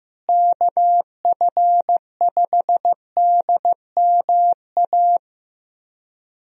Morse code, heard as a keyed tone, about 15 wpm, 700 Hz. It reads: KF5DMA
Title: USA Call Signs - 6 Characters - Repeated 3x - 15wpm